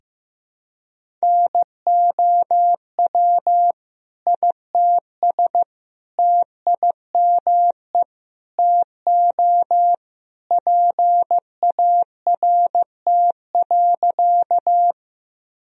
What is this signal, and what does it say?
Morse code, heard as a keyed tone, about 15 wpm, 700 Hz. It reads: NOW ITS TIME TO PART.
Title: Sentences from Top 1000 Words - Review Part 1 - 15wpm